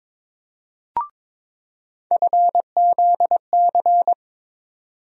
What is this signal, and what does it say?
Morse code, heard as a keyed tone, about 22 wpm, 700 Hz. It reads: FZC
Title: USA Call Sign Postfixes - 3 Letters 22wpm